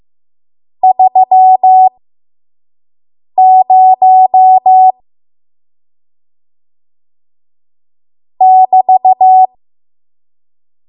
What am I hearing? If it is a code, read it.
Morse code, heard as a keyed tone, about 15 wpm, spaced as for 5 wpm, 750 Hz. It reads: 30 <BT>
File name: Morse / 05WPM